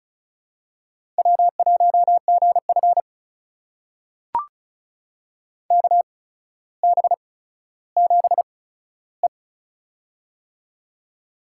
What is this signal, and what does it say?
Morse code, heard as a keyed tone, about 36 wpm, spaced as for 15 wpm, 700 Hz. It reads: W1GF KB7E
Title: USA Call Signs - 8x Character Spacing - 35wpm